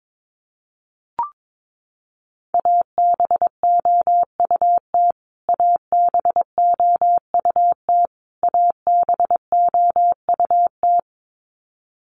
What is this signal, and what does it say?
Morse code, heard as a keyed tone, about 22 wpm, 700 Hz. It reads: ABOUT ABOUT ABOUT
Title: Top 500 Words Repeated 22wpm